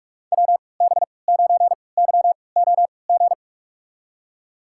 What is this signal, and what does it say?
Morse code, heard as a keyed tone, about 46 wpm, spaced as for 25 wpm, 700 Hz. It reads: WB9YOG
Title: USA Call Signs - 6 Characters - 3x Character Spacing - 45wpm